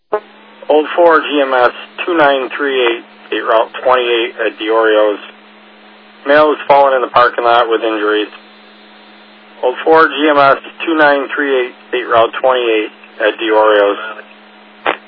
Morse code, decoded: none

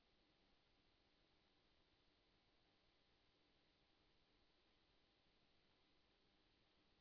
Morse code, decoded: none

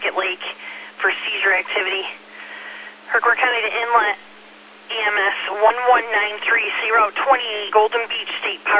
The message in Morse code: none

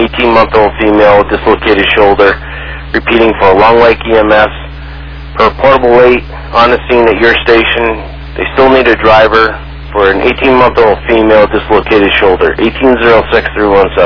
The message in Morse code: none

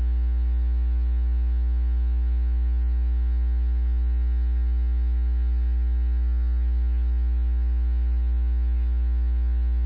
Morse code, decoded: none

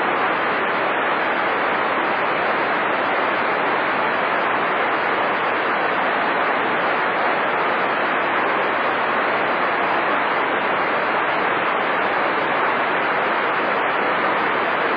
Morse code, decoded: none

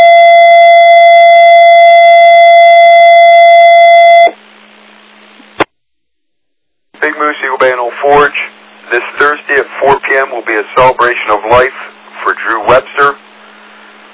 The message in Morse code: none

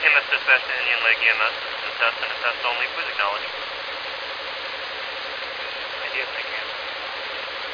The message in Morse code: none

